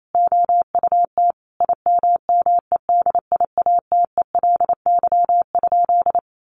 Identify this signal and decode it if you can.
Morse code, heard as a keyed tone, about 28 wpm, 700 Hz. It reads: OUT IMMEDIATELY?
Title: Sentences from Top 1300 Words 28wpm